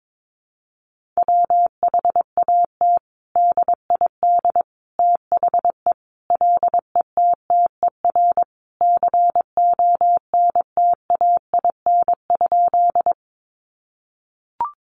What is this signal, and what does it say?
Morse code, heard as a keyed tone, about 22 wpm, 700 Hz. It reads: WHAT DID THE LETTER CONTAIN?
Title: Sentences from Top 500 Words 22wpm